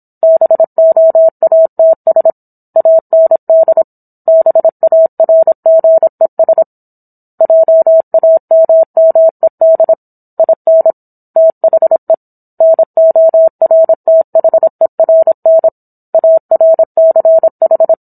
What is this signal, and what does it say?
Morse code, heard as a keyed tone, about 27 wpm, 650 Hz. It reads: BOATS AND BARGES JAMMED IN THE NORTHERN ARCH